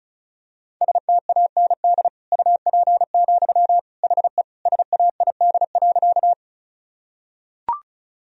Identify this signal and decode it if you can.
Morse code, heard as a keyed tone, about 34 wpm, 690 Hz. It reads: STAND UP, HE SAID.
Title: Sentences from Top 1000 Words - Review Part 1 - 35wpm